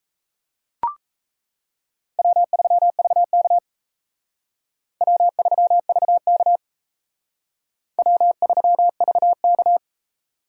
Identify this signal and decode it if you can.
Morse code, heard as a keyed tone, about 35 wpm, 700 Hz. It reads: W3VK W3VK W3VK